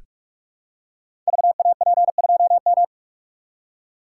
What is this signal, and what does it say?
Morse code, heard as a keyed tone, about 44 wpm, 700 Hz. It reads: VAP2M
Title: Code Groups - 5 Letter:Numbers - 45wpm